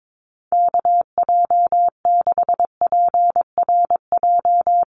Morse code, heard as a keyed tone, about 22 wpm, 700 Hz. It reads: KJ6PRJ
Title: USA Call Signs 22wpm